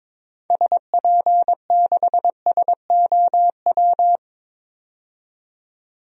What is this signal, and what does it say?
Morse code, heard as a keyed tone, about 22 wpm, 700 Hz. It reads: SP6SOW
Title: DXCC-100 Call Signs - 6 Characters - 22wpm